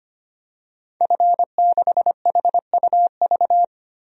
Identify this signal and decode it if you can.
Morse code, heard as a keyed tone, about 25 wpm, 700 Hz. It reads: F6HUV
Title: DXCC-100 Call Signs - 5 Characters - 25wpm